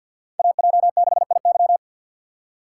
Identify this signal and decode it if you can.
Morse code, heard as a keyed tone, about 46 wpm, 700 Hz. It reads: AJ6IY
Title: USA Call Signs - 6x Character Spacing - 50wpm